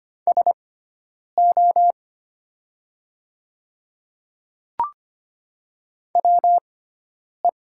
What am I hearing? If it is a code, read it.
Morse code, heard as a keyed tone, about 25 wpm, spaced as for 9 wpm, 700 Hz. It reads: SO WE
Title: 6x Character Spacing - Top 100 Words - 25wpm